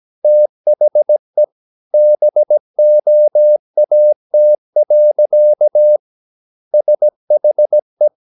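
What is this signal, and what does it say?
Morse code, heard as a keyed tone, about 17 wpm, 600 Hz. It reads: THE BOAT. SHE
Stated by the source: 17 wpm Morse practice sending the English text of a novel.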